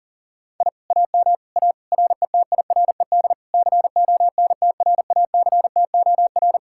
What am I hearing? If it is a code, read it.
Morse code, heard as a keyed tone, about 48 wpm, 700 Hz. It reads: I AM A RETIRED CONTRACTOR
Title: Speed-Racing Common QSO Phrases - 40wpm